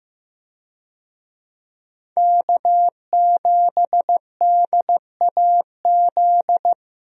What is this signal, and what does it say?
Morse code, heard as a keyed tone, about 15 wpm, 700 Hz. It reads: K7DAZ
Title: USA Call Signs - 5 Characters - 15wpm